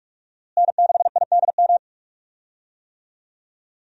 Morse code, heard as a keyed tone, about 46 wpm, 700 Hz. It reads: N6IDM
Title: USA Call Signs - 5 Characters - Repeated 3x - 45wpm